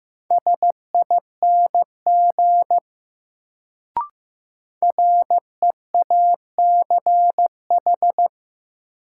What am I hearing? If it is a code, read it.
Morse code, heard as a keyed tone, about 15 wpm, 700 Hz. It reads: SING REACH
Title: Sentences from Top 400 Words - New Word Drill - 15wpm